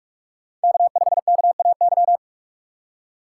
Morse code, heard as a keyed tone, about 42 wpm, 700 Hz. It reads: K5KAY
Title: USA Call Signs - 5 Characters - 4x Character Spacing - 45wpm